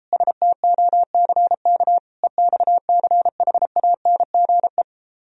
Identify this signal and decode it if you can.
Morse code, heard as a keyed tone, about 29 wpm, 700 Hz. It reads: STOCK EXCHANGE